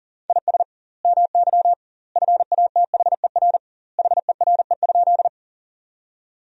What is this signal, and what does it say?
Morse code, heard as a keyed tone, about 40 wpm, 700 Hz. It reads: IS MY FATHER HERE?
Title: 2x Word-Spacing - Sentences from Top 300 Words - 40wpm